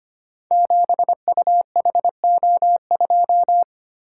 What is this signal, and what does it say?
Morse code, heard as a keyed tone, about 25 wpm, 700 Hz. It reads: 7UHO2
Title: Code Groups - 5 Letter:Numbers - 25wpm